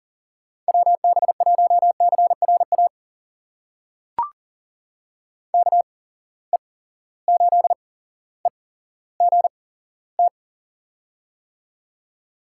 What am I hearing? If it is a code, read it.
Morse code, heard as a keyed tone, about 39 wpm, spaced as for 17 wpm, 700 Hz. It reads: WB1CRA KE8EGT
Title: USA Call Signs - 6 Characters - 8x Character Spacing - 40wpm